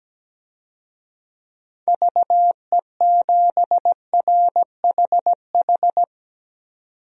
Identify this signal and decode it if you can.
Morse code, heard as a keyed tone, about 17 wpm, 700 Hz. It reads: VE7RHH
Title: DXCC-100 Call Signs - 6 Characters - 17wpm